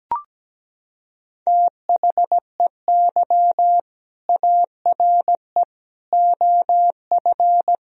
Morse code, heard as a keyed tone, about 17 wpm, 700 Hz. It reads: THEY ARE OF
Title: Sentences from Top 1000 Words - Review Part 4 - 17wpm